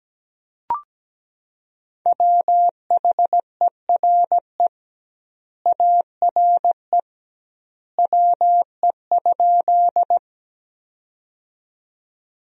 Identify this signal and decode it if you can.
Morse code, heard as a keyed tone, about 17 wpm, 700 Hz. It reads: WHERE ARE WE?
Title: 2x Word-Spacing - Sentences from Top 200 Words - 17wpm